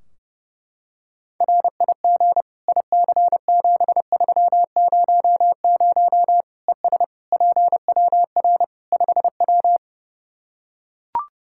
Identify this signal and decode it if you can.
Morse code, heard as a keyed tone, about 30 wpm, 700 Hz. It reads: RIG IC7300 ES PWR 5W
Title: Practice QSO - Exchange 2 - Info - 30wpm